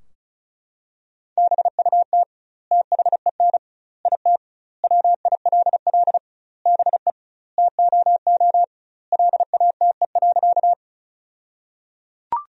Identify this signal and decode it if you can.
Morse code, heard as a keyed tone, about 36 wpm, 700 Hz. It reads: BUT THEN IT WILL BE TOO LATE.
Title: 2x Word-Spacing - Sentences from Top 300 Words - 35wpm